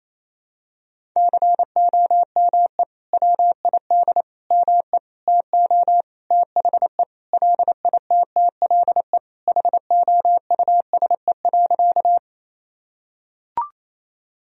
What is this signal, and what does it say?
Morse code, heard as a keyed tone, about 28 wpm, 700 Hz. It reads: COME WID ME TO THE LITTLE HOUSE.
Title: Sentences from Top 200 Words and Abbreviations 28wpm